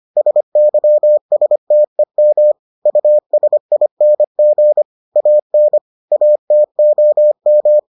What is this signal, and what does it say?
Morse code, heard as a keyed tone, about 25 wpm, 600 Hz. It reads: SYSTEM USING AN ATOM